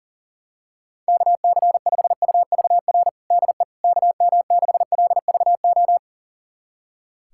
Morse code, heard as a keyed tone, about 40 wpm, 700 Hz. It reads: KC5UVR DE KM6LVO
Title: USA Callsign DE Callsign 40wpm